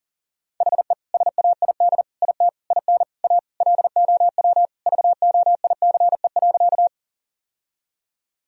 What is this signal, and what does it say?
Morse code, heard as a keyed tone, about 40 wpm, 700 Hz. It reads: HE SAID IT IN A LOW VOICE.
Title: Sentences from Top 1000 Words - Review Part 1 - 40wpm